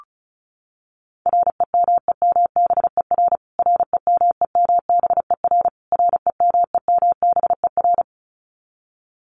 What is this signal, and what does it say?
Morse code, heard as a keyed tone, about 34 wpm, 700 Hz. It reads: REMEMBER REMEMBER REMEMBER
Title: Top 500 Words Repeated 35wpm